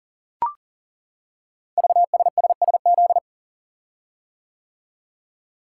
Morse code, heard as a keyed tone, about 38 wpm, 690 Hz. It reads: VSSSZ
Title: Code Groups - 5 Letters - 40wpm